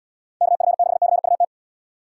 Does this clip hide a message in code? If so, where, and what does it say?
Yes: Morse code from 0.4 to 1.4 s, heard as a keyed tone, about 79 wpm, 700 Hz. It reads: KF6CUT